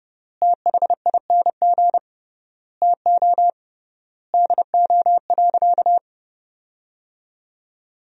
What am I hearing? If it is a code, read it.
Morse code, heard as a keyed tone, about 30 wpm, 700 Hz. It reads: THING TO DO.
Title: 3x Word-Spacing - Sentences from Top 100 Words - 30wpm